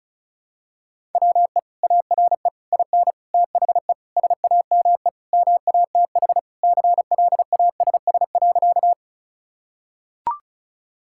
Morse code, heard as a keyed tone, about 36 wpm, 700 Hz. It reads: WE ARE IN THE SAME MATH CLASS.
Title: Sentences from Top 1000 Words - Review Part 6 - 35wpm